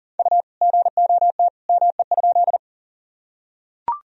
Morse code, heard as a keyed tone, about 38 wpm, 700 Hz. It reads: U GOT ME?
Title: Sentences from Top 300 Words and Abbreviations 40wpm